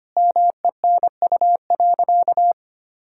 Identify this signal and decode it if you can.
Morse code, heard as a keyed tone, about 25 wpm, 700 Hz. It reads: MENU.
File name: practice / 25wpm / Sentences with 4-Word Phrases - 25wpm